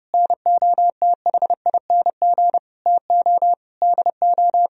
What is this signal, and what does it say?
Morse code, heard as a keyed tone, about 29 wpm, 700 Hz. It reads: NOTHING TO DO